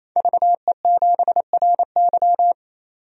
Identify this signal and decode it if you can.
Morse code, heard as a keyed tone, about 28 wpm, 700 Hz. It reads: VE7RY
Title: DXCC-100 Call Signs - 5 Characters - 28wpm